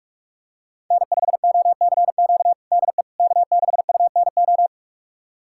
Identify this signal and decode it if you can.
Morse code, heard as a keyed tone, about 46 wpm, 700 Hz. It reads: N5OCQ DE K6UNO
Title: USA Callsign DE Callsign - 5 Characters - 45wpm